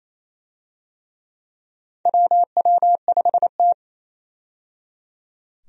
Morse code, heard as a keyed tone, about 28 wpm, 700 Hz. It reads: WW5T
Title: USA Call Signs - 4 Characters - 28wpm